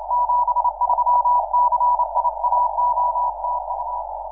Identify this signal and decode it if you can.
Morse code, heard as a keyed tone, about 28 wpm, 980 Hz, with noise. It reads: 73 CIAO TU
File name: iw0ffk / 2006 06 12 un7qx iw0ffk